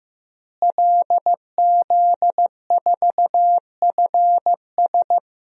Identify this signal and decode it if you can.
Morse code, heard as a keyed tone, about 15 wpm, 700 Hz. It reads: LZ4FS
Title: DXCC-100 Call Signs - 5 Characters - Repeated 3x - 15wpm